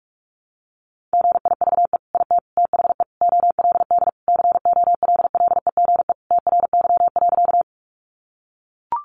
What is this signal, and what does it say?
Morse code, heard as a keyed tone, about 44 wpm, 700 Hz. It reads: GIVE IT THE OLD COLLEGE TRY.
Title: Sentences from Top 1300 Words 45wpm